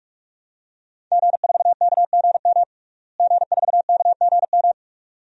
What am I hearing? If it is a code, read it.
Morse code, heard as a keyed tone, about 46 wpm, 700 Hz. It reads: G4KGM G4KGM
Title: DXCC-100 Call Signs - 5 Characters - Repeated 2x - 45wpm